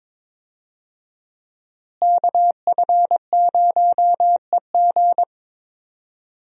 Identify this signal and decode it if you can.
Morse code, heard as a keyed tone, about 22 wpm, 700 Hz. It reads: KF0EG
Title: USA Call Signs - 5 Characters - Repeated 2x - 22wpm